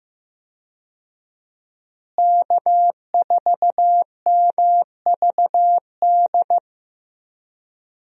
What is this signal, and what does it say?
Morse code, heard as a keyed tone, about 15 wpm, 700 Hz. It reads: K4MVD
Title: USA Call Signs - 5 Characters - Repeated 2x - 15wpm